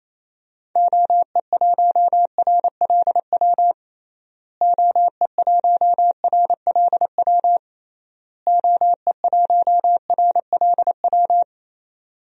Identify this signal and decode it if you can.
Morse code, heard as a keyed tone, about 28 wpm, 700 Hz. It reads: OE1RLW OE1RLW OE1RLW
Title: DXCC-100 Call Signs - 6 Characters - Repeated 3x - 28wpm